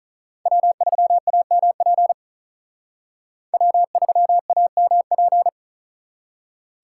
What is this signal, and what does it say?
Morse code, heard as a keyed tone, about 45 wpm, 700 Hz. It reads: W3AMP W3AMP